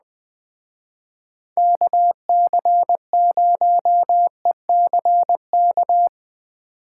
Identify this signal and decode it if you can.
Morse code, heard as a keyed tone, about 20 wpm, 700 Hz. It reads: KC0ECK